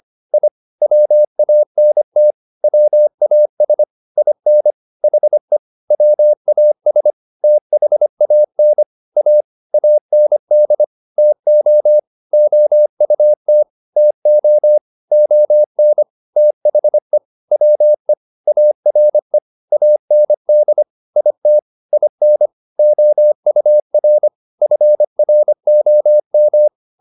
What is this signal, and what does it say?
Morse code, heard as a keyed tone, about 25 wpm, 600 Hz. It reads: I WANT WAS IN HE WAS THAN A AND TO OUT TO ON THE WE ARE AND IT IN OUR FROM